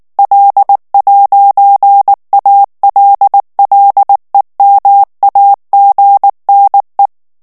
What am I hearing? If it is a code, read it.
Morse code, heard as a keyed tone, about 19 wpm, 800 Hz. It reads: L'ALLEMAGNE
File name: morse mp3 / FRANCAIS TEXTE CLAIR 1 VITESSE 18 MOTS MINUTES